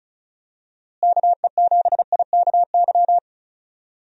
Morse code, heard as a keyed tone, about 36 wpm, 700 Hz. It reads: KE7IKY
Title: USA Call Signs - 6 Characters - Repeated 2x - 35wpm